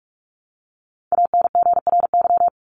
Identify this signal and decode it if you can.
Morse code, heard as a keyed tone, about 44 wpm, 700 Hz. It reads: ANGRY
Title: Top 1751-2000 Words 45wpm